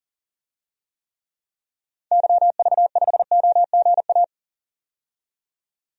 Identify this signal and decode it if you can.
Morse code, heard as a keyed tone, about 40 wpm, 700 Hz. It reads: YV5OGA
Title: DXCC-100 Call Signs - 6 Characters - 40wpm